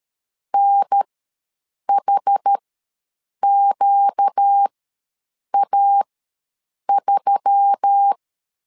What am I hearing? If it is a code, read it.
Morse code, heard as a keyed tone, about 13 wpm, spaced as for 7 wpm, 800 Hz. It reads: NHQA3
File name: LEZIONI / 4 Q1S3 ws 12wpm 2038